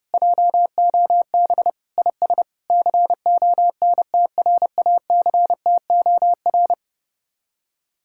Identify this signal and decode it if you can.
Morse code, heard as a keyed tone, about 31 wpm, 700 Hz. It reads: JOB IS CONTRACTOR